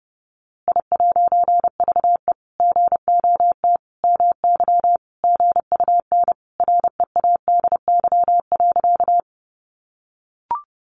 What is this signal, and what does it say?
Morse code, heard as a keyed tone, about 30 wpm, 700 Hz. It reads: I'VE GOT MY GUN READY.
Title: A Sound Of Thunder 30wpm